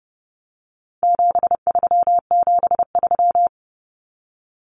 Morse code, heard as a keyed tone, about 30 wpm, 700 Hz. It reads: 7373
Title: Common Cut Numbers - 4-Digit 30wpm